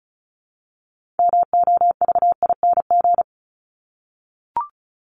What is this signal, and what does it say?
Morse code, heard as a keyed tone, about 34 wpm, 700 Hz. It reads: MOVING
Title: ING Suffix - Top 100 Words with Base Words 35wpm